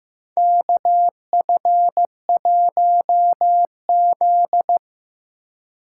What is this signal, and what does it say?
Morse code, heard as a keyed tone, about 15 wpm, 700 Hz. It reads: KF1Z
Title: USA Call Signs - 4 Characters - 2x Character Spacing - 15wpm